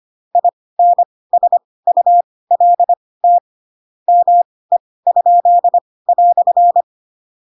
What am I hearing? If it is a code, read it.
Morse code, heard as a keyed tone, about 25 wpm, spaced as for 18 wpm, 700 Hz. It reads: INSULT ME?"
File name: Pinocchio / Pinocchio ch2 12 25 ik1hge